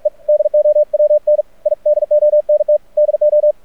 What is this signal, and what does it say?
Morse code, heard as a keyed tone, about 44 wpm, 600 Hz. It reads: E DOWN I DONT DO
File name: archived / CRSnet